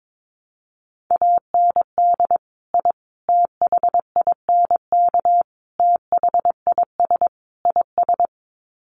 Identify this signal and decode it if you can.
Morse code, heard as a keyed tone, about 22 wpm, 700 Hz. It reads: AND I THINK THIS IS
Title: Sets of 5 Words Encore 22wpm